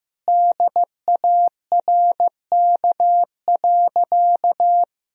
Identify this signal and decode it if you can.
Morse code, heard as a keyed tone, about 15 wpm, 700 Hz. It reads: DARK.
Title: Sentences from Top 500 Words and Abbreviations 15wpm